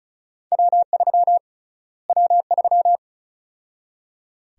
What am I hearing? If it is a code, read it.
Morse code, heard as a keyed tone, about 36 wpm, 700 Hz. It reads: W3 W3
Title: USA Call Sign Prefixes - Repeated 2x - 35wpm